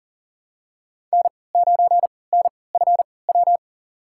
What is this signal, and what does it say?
Morse code, heard as a keyed tone, about 39 wpm, spaced as for 23 wpm, 700 Hz. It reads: N9NFW
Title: USA Call Signs - 3x Character Spacing - 40wpm